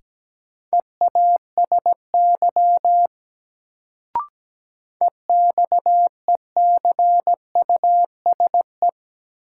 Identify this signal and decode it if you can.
Morse code, heard as a keyed tone, about 17 wpm, 700 Hz. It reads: EASY EXECUSE
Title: Sentences from Top 700 Words - New Word Drill - 17wpm